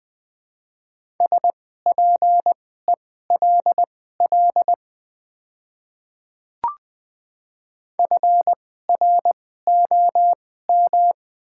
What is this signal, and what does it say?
Morse code, heard as a keyed tone, about 20 wpm, spaced as for 14 wpm, 700 Hz. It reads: SPELL FROM
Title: 2x Character Spacing - Top 300 Words - 20wpm